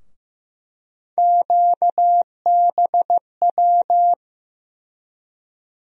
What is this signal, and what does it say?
Morse code, heard as a keyed tone, about 15 wpm, 700 Hz. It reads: QBW
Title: USA Call Sign Postfixes - 3 Letters 15wpm